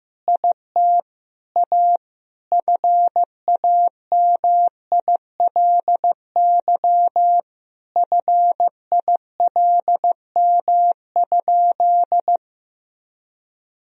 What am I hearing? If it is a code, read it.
Morse code, heard as a keyed tone, about 16 wpm, 700 Hz. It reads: IT A FAMILY FILM?
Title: Sentences from Top 1000 Words - Review Part 3 - 15wpm